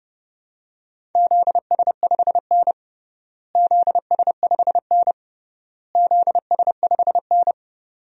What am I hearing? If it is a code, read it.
Morse code, heard as a keyed tone, about 32 wpm, 700 Hz. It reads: ZS5N ZS5N ZS5N
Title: DXCC-100 Call Signs - 4 Characters - Repeated 3x - 30wpm